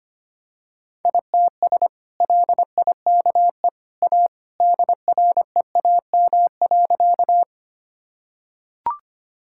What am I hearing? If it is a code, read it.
Morse code, heard as a keyed tone, about 25 wpm, 700 Hz. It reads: ITS LIKE A DREAM.
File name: practice / 25wpm / Sentences with 2-Word Phrases - 25wpm